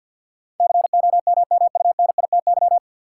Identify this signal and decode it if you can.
Morse code, heard as a keyed tone, about 53 wpm, 700 Hz. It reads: COMMUNITY
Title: Sentences from Top 1000 Words - Review Part 4 - New Word Drill - 50wpm